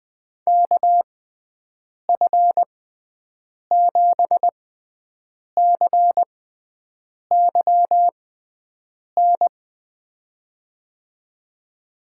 Morse code, heard as a keyed tone, about 20 wpm, spaced as for 7 wpm, 700 Hz. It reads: KF7CYN
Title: USA Call Signs - 6 Characters - 6x Character Spacing - 20wpm